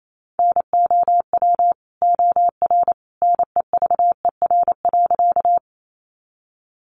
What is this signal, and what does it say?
Morse code, heard as a keyed tone, about 28 wpm, 700 Hz. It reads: NOW OR NEVER.
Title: Sentences from Top 300 Words 28wpm